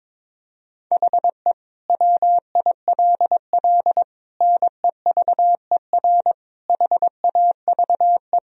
Morse code, heard as a keyed tone, about 22 wpm, 700 Hz. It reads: HE WILL NEVER HAVE
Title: Sentences from Top 1000 Words - Review Part 3 - 22wpm